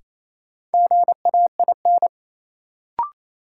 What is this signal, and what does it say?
Morse code, heard as a keyed tone, about 28 wpm, 700 Hz. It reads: GAIN
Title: Sentences from Top 1000 Words - New Word Drill - 28wpm